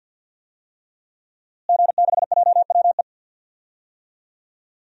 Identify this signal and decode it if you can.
Morse code, heard as a keyed tone, about 48 wpm, 700 Hz. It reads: G6JWE
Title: DXCC-100 Call Signs - 5 Characters - Repeated 3x - 50wpm